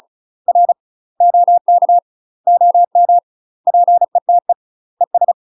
Morse code, heard as a keyed tone, about 36 wpm, 700 Hz. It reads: R OK OM PETE ES
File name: UK MASTER QSOs TRAINING / 35WPM / UK MASTER QSOs 0011